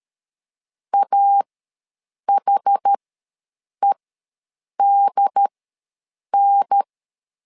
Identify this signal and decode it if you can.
Morse code, heard as a keyed tone, about 13 wpm, spaced as for 7 wpm, 800 Hz. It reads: AHEDN
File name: LEZIONI / 3 MWRO / 3 MWR ws 12wpm 1809